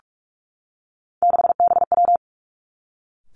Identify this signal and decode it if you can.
Morse code, heard as a keyed tone, about 46 wpm, 700 Hz. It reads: 6BW
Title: Three Challenging Letter-Number Combinations 45wpm